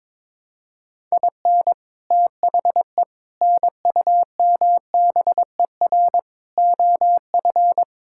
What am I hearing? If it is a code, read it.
Morse code, heard as a keyed tone, about 22 wpm, 700 Hz. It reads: IN THE NUMBER OF
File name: practice / 22wpm / Sentences with 4-Word Phrases - 22wpm